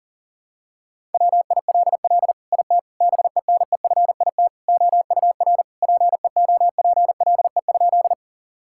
Morse code, heard as a keyed tone, about 40 wpm, 700 Hz. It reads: WILL IT BENEFIT OUR PEOPLE?
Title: Sentences from Top 1000 Words - Review Part 6 - 40wpm